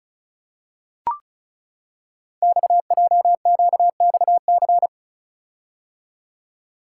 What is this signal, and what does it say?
Morse code, heard as a keyed tone, about 36 wpm, 690 Hz. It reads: XJQXC
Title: Code Groups - 5 Letters - 35wpm